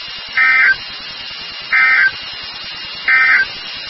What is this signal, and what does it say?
Emergency Alert System (SAME) tones, heard as harsh buzzing bursts. It is heard from 0.3 to 3.5 s.
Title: NOAA Weather Radio WXK36